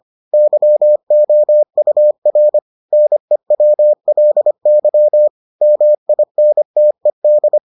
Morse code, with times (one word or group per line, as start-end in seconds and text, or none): 0.3-2.6 YOUR
2.9-5.3 NEWLY
5.6-7.6 MINTED